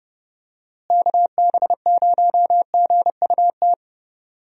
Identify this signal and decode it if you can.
Morse code, heard as a keyed tone, about 30 wpm, 700 Hz. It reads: KB0GUT
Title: USA Call Signs - 6 Characters - Repeated 3x - 30wpm